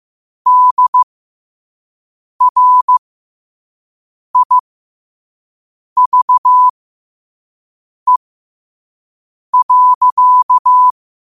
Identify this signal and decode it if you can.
Morse code, heard as a keyed tone, about 15 wpm, spaced as for 5 wpm, 1000 Hz. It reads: DRIVE.
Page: New quotes every day in morse code at 5 Words per minute.